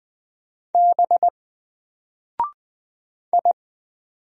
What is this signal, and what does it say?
Morse code, heard as a keyed tone, about 20 wpm, 700 Hz. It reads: B I
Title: Single Character (Letter:Number:Punctuation:Prosign) Alt 20wpm